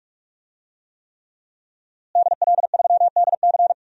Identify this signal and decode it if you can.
Morse code, heard as a keyed tone, about 44 wpm, 700 Hz. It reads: DL3DC
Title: DXCC-100 Call Signs - 5 Characters - Repeated 3x - 45wpm